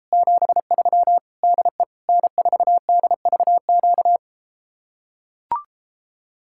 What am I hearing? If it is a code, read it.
Morse code, heard as a keyed tone, about 34 wpm, 700 Hz. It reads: 73 DE N4DVQ